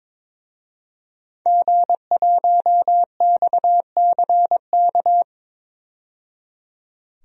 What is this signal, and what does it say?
Morse code, heard as a keyed tone, about 22 wpm, 700 Hz. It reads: G1XCK
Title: DXCC-100 Call Signs - 5 Characters - 22wpm